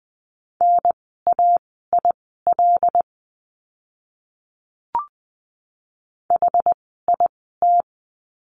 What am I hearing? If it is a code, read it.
Morse code, heard as a keyed tone, about 20 wpm, spaced as for 14 wpm, 700 Hz. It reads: NAIL HIT
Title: 075 Lesson 13f - Letter L - Words 20wpm